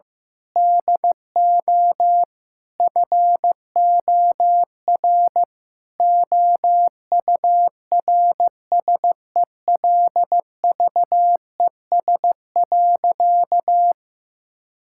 Morse code, heard as a keyed tone, about 16 wpm, 700 Hz. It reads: DO FOR OURSELVES.
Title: Sentences from Top 1600 Words 15wpm